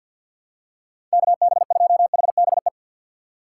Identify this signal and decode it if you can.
Morse code, heard as a keyed tone, about 48 wpm, 700 Hz. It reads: KB2HBE
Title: USA Call Signs - 4x Character Spacing - 50wpm